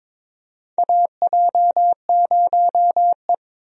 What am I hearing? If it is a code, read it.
Morse code, heard as a keyed tone, about 22 wpm, 700 Hz. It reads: AJ0E
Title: USA Call Signs - 4 Characters - 22wpm